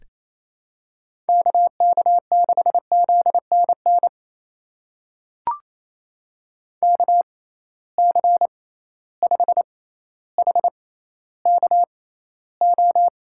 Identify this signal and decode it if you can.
Morse code, heard as a keyed tone, about 27 wpm, spaced as for 14 wpm, 700 Hz. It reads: KK6ZNN KC5HKO
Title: USA Call Signs - 6 Characters - 6x Character Spacing - 28wpm